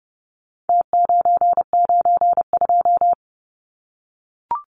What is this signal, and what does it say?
Morse code, heard as a keyed tone, about 30 wpm, 700 Hz. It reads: T992
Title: Numbers - Contest Style - 4-Digit 30wpm